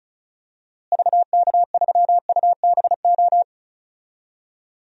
Morse code, heard as a keyed tone, about 35 wpm, 700 Hz. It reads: VK3UBO
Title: DXCC-100 Call Signs - 6 Characters - Repeated 2x - 35wpm